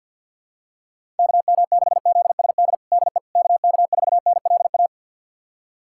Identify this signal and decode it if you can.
Morse code, heard as a keyed tone, about 48 wpm, 700 Hz. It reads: KM6ZSD DE KK4NLA
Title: USA Callsign DE Callsign - 6 Characters - 50wpm